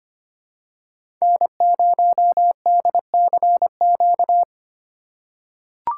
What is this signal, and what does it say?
Morse code, heard as a keyed tone, about 25 wpm, 700 Hz. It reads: N0DCQ
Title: USA Call Signs - 3x Character Spacing - 25wpm